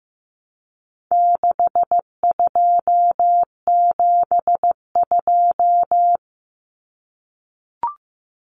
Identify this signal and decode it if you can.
Morse code, heard as a keyed tone, about 15 wpm, 700 Hz. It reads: 6272
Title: Common Cut Numbers - 4-Digit 15wpm